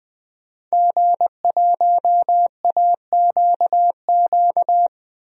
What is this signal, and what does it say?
Morse code, heard as a keyed tone, about 20 wpm, 700 Hz. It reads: G1AQQ
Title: DXCC-100 Call Signs - 5 Characters - 20wpm